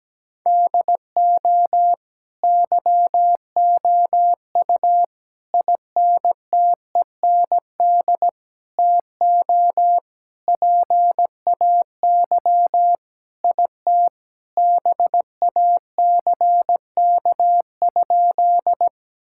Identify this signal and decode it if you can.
Morse code, heard as a keyed tone, about 17 wpm, 700 Hz. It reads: DO YOU INTEND TO PAY IT BACK?
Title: Sentences from Top 1600 Words 17wpm